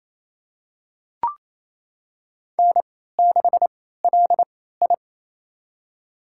Morse code, heard as a keyed tone, about 27 wpm, spaced as for 16 wpm, 700 Hz. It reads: N6LI
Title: USA Call Signs - 4 Characters - 3x Character Spacing - 28wpm